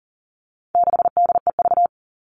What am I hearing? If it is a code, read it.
Morse code, heard as a keyed tone, about 40 wpm, 700 Hz. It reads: 6DEV
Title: All Cut Numbers - 4-Digit 40wpm